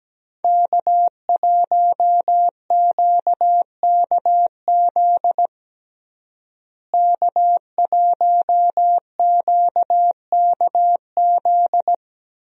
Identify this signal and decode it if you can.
Morse code, heard as a keyed tone, about 17 wpm, 700 Hz. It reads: K1QKZ K1QKZ
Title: USA Call Signs - 5 Characters - Repeated 2x - 17wpm